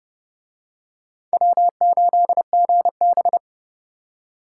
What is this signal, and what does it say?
Morse code, heard as a keyed tone, about 28 wpm, 700 Hz. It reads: W8GB